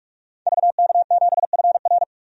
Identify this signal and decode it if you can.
Morse code, heard as a keyed tone, about 46 wpm, 700 Hz. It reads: VK7FR
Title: DXCC-100 Call Signs - 5 Characters - 45wpm